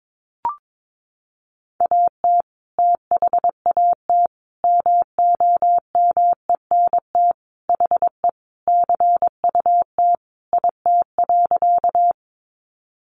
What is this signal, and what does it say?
Morse code, heard as a keyed tone, about 22 wpm, 700 Hz. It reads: AT THAT MOMENT HE CUT IT.
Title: Sentences from Top 1000 Words 22wpm